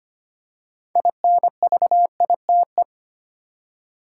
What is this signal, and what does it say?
Morse code, heard as a keyed tone, about 25 wpm, 700 Hz. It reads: INVITE
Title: Six Letter Words - 25wpm